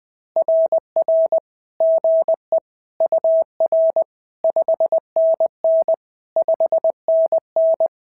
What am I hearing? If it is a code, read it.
Morse code, heard as a keyed tone, about 20 wpm, 650 Hz. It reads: RR GE UR 5NN 5NN
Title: SOTA QSO Practice 20wpm